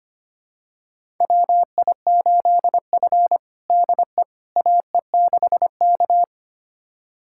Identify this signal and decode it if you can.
Morse code, heard as a keyed tone, about 25 wpm, 700 Hz. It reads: WI8F DE AE6K
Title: USA Callsign DE Callsign - 4 Characters - 25wpm